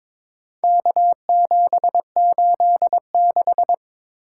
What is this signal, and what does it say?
Morse code, heard as a keyed tone, about 22 wpm, 700 Hz. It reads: K786
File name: practice / 22wpm / Four Letter-Number Combinations - 22wpm